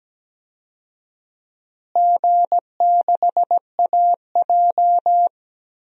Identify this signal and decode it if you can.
Morse code, heard as a keyed tone, about 17 wpm, 700 Hz. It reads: G6AJ
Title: DXCC-100 Call Signs - 4 Characters - Repeated 2x - 17wpm